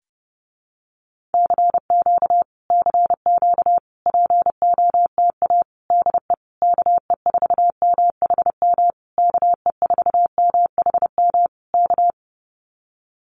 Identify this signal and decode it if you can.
Morse code, heard as a keyed tone, about 30 wpm, 700 Hz. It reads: CQ CQ POTA DE KE4MHM KE4MHM K
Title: Common QSO Phrases Encore 30wpm